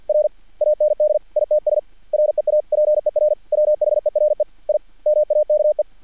Morse code, heard as a keyed tone, about 50 wpm, 600 Hz. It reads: R AND ITS REAPER WHERE I MAKE
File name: archived / CRSnet